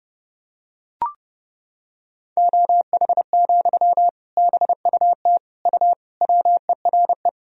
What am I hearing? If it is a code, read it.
Morse code, heard as a keyed tone, about 30 wpm, 700 Hz. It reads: OH, BUT U WERE
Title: Sentences from Top 500 Words and Abbreviations 30wpm